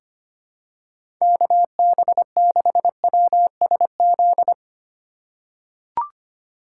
Morse code, heard as a keyed tone, about 25 wpm, 700 Hz. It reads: KB6WSZ
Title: USA Call Signs - 6 Characters - 2x Character Spacing - 25wpm